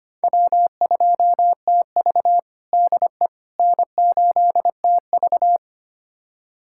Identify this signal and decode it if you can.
Morse code, heard as a keyed tone, about 25 wpm, 700 Hz. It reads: W2TV DE N8TV
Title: USA Callsign DE Callsign - 4 Characters - 25wpm